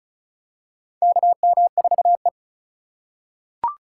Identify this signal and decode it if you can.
Morse code, heard as a keyed tone, about 34 wpm, 700 Hz. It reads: KM4E
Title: USA Call Signs - 3x Character Spacing - 35wpm